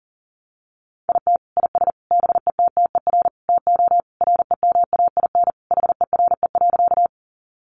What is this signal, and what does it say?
Morse code, heard as a keyed tone, about 40 wpm, 700 Hz. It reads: IT IS BETTER TO REMAIN HERE.
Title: Sentences from Top 1200 Words 40wpm